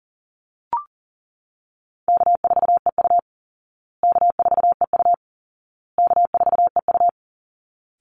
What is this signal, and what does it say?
Morse code, heard as a keyed tone, about 40 wpm, 700 Hz. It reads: K4EU K4EU K4EU
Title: Easy Call Signs Repeated 3x - 40wpm